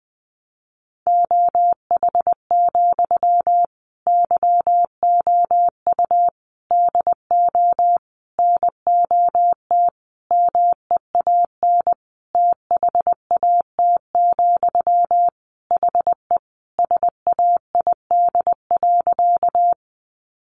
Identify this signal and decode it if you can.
Morse code, heard as a keyed tone, about 20 wpm, 700 Hz. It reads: OH, YOU DO NOT MEAN THAT, HE SAID.
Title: Sentences from Top 600 Words 20wpm